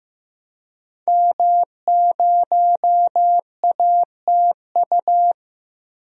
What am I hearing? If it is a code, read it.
Morse code, heard as a keyed tone, about 15 wpm, 700 Hz. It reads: M0ATU
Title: DXCC-100 Call Signs - 5 Characters - Repeated 3x - 15wpm